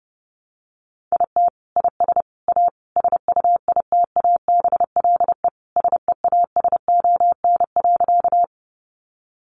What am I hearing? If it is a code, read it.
Morse code, heard as a keyed tone, about 30 wpm, 700 Hz. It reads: IT IS A SUITABLE SEASON.
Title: Sentences from Top 1400 Words 30wpm